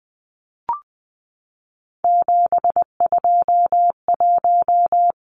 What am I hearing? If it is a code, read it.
Morse code, heard as a keyed tone, about 20 wpm, 700 Hz. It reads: 721
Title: Numbers - 3-Digit 20wpm